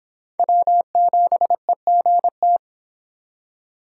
Morse code, heard as a keyed tone, about 24 wpm, 700 Hz. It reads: W7EGT